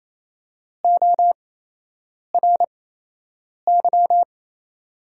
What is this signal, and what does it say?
Morse code, heard as a keyed tone, about 28 wpm, spaced as for 8 wpm, 700 Hz. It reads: ORY
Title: 8x Character Spacing - Top 500 Words - 28wpm